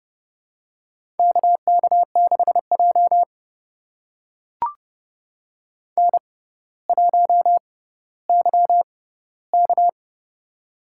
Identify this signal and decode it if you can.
Morse code, heard as a keyed tone, about 28 wpm, spaced as for 15 wpm, 700 Hz. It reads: KK6J N1YK
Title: USA Call Signs - 4 Characters - 6x Character Spacing - 30wpm